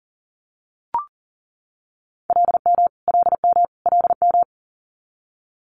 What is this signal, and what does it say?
Morse code, heard as a keyed tone, about 37 wpm, 700 Hz. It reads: LM LM LM
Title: Multi-letter Phonemes Repeated 3x 40wpm